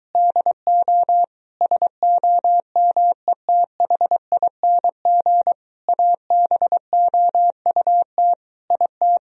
Morse code, heard as a keyed tone, about 23 wpm, 700 Hz. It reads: DO SOMETHING ABOUT IT